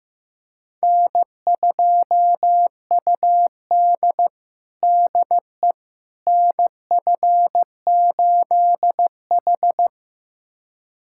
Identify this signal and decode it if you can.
Morse code, heard as a keyed tone, about 15 wpm, 700 Hz. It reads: N2UD DE NF8H
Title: USA Callsign DE Callsign - 4 Characters - 15wpm